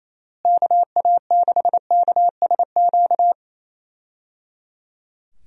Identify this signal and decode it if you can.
Morse code, heard as a keyed tone, about 28 wpm, 700 Hz. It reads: KA6KSQ
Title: USA Call Signs - 6 Characters - Repeated 3x - 28wpm